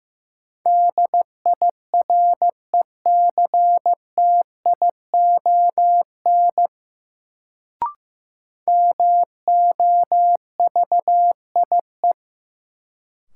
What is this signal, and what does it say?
Morse code, heard as a keyed tone, about 15 wpm, 700 Hz. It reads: DIRECTION MOVIE
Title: Sentences from Top 1000 Words - Review Part 5 - New Word Drill - 15wpm